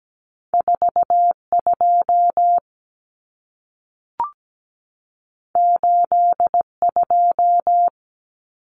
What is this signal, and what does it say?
Morse code, heard as a keyed tone, about 17 wpm, 700 Hz. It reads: 42 82
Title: Common Cut Numbers - 2-Digit 17wpm